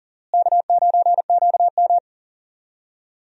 Morse code, heard as a keyed tone, about 40 wpm, 700 Hz. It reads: K9QM
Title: USA Call Signs - 4 Characters - Repeated 2x - 40wpm